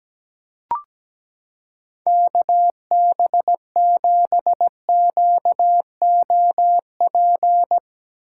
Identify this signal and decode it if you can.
Morse code, heard as a keyed tone, about 17 wpm, 700 Hz. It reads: KB7QOP
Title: K1USN SST QSO Practice - 17wpm